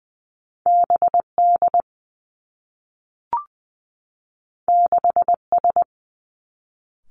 Morse code, heard as a keyed tone, about 20 wpm, 700 Hz. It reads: B D 6 S
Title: Two Challenging Letter-Number Combinations 20wpm